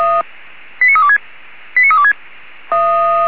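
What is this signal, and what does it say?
Five-tone selective call (ZVEI), heard as quick runs of five tones, tones of about 70 ms. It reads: 89326 89326
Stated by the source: ZVEI five-tone selective call 89330 received by SDR.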